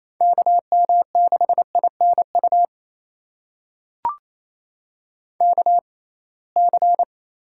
Morse code, heard as a keyed tone, about 28 wpm, spaced as for 21 wpm, 700 Hz. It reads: KM6INU KC
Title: USA Call Signs - 6 Characters - 6x Character Spacing - 28wpm